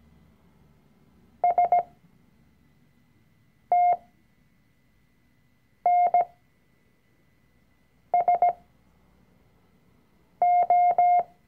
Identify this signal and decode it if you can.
Morse code, heard as a keyed tone, about 17 wpm, spaced as for 4 wpm, 710 Hz. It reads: STNSO
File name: CW-Online-Kurs / Anfänger / Lektion 004